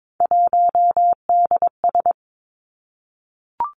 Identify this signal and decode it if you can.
Morse code, heard as a keyed tone, about 22 wpm, 700 Hz. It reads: 1DS
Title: Three Challenging Letter-Number Combinations 22wpm